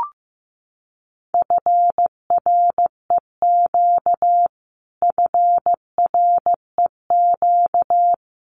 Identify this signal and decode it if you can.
Morse code, heard as a keyed tone, about 15 wpm, 700 Hz. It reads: FREQ FREQ
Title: Abbreviations Repeated 3x - 15wpm